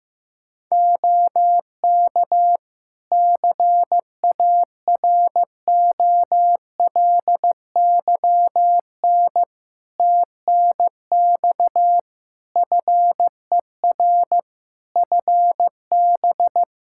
Morse code, heard as a keyed tone, about 16 wpm, 700 Hz. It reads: OK CAROLYN TNX FER FB
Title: Practice QSO - Exchange 4 - Ending - 15wpm